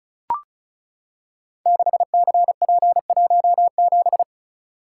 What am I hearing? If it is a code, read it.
Morse code, heard as a keyed tone, about 36 wpm, 690 Hz. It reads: 6CP17
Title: Code Groups - 5 Letter:Numbers - 35wpm